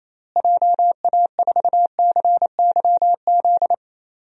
Morse code, heard as a keyed tone, about 27 wpm, 700 Hz. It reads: JA4CYZ
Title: DXCC-100 Call Signs - 6 Characters - Repeated 3x - 28wpm